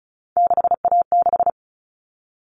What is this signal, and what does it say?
Morse code, heard as a keyed tone, about 34 wpm, 700 Hz. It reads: 6A6
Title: All Cut Numbers - 3-Digit 35wpm